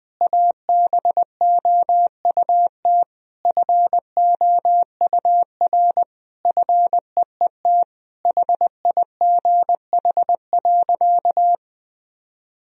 Morse code, heard as a keyed tone, about 20 wpm, 700 Hz. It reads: ABOUT FOUR FEET HIGH.
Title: Sentences from Top 1000 Words - Review Part 1 - 20wpm